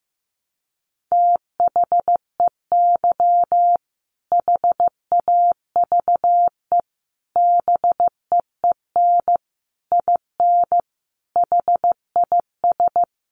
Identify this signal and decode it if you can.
Morse code, heard as a keyed tone, about 15 wpm, 700 Hz. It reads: THEY HAVE BEEN IN HIS
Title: Sentences from Top 700 Words 15wpm